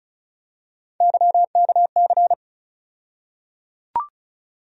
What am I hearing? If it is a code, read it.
Morse code, heard as a keyed tone, about 34 wpm, 700 Hz. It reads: YKC
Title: USA Call Sign Postfixes - 3 Letters 35wpm